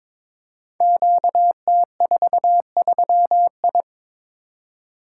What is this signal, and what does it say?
Morse code, heard as a keyed tone, about 22 wpm, 700 Hz. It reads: QT43I
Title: Code Groups - 5 Letter:Numbers - 22wpm